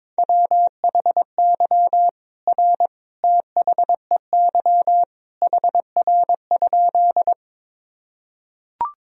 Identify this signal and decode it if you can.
Morse code, heard as a keyed tone, about 21 wpm, 700 Hz. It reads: WHY R THEY HR?
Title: Sentences from Top 200 Words and Abbreviations 22wpm